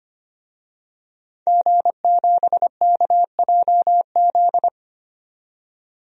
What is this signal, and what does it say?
Morse code, heard as a keyed tone, about 25 wpm, 700 Hz. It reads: G7KJZ
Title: DXCC-100 Call Signs - 5 Characters - 25wpm